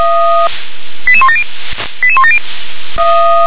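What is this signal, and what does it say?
Five-tone selective call (ZVEI), heard as quick runs of five tones, tones of about 70 ms. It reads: 88170 88170
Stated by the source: ZVEI five-tone call 88101, received on SDR.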